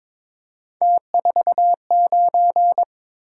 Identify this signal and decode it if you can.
Morse code, heard as a keyed tone, about 26 wpm, 700 Hz. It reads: T49
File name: Speed-Racing - Three Letter-Number Combinations - 22wpm